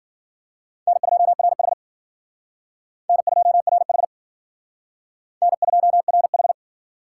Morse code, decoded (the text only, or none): N2RH N2RH N2RH